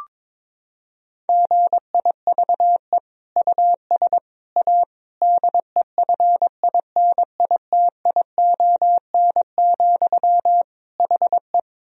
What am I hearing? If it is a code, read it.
Morse code, heard as a keyed tone, about 22 wpm, 700 Hz. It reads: GIVE US A DEFINITION, HE
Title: Sentences from Top 1000 Words - Review Part 4 - 22wpm